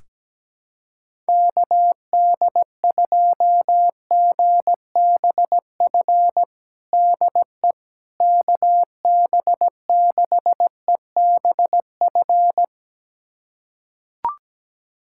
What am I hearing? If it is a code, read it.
Morse code, heard as a keyed tone, about 17 wpm, 700 Hz. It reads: KD2GBF DE KB6EBF
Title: USA Callsign DE Callsign 17wpm